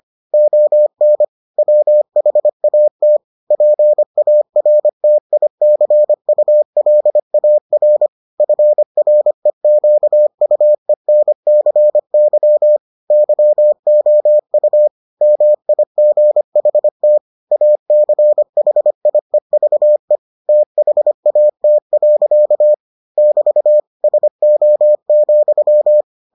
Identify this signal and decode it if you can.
Morse code, heard as a keyed tone, about 25 wpm, 600 Hz. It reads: ON WHAT PARTICULAR FREQUENCY YOU MIGHT ACHIEVE THAT. <BT> SO,